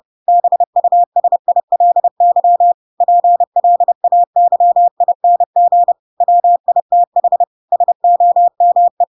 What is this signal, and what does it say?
Morse code, heard as a keyed tone, about 30 wpm, 700 Hz. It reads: BUSILY PLAYING WITH SOME
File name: Pinocchio / Pinocchio ch7 30 30 ik1hge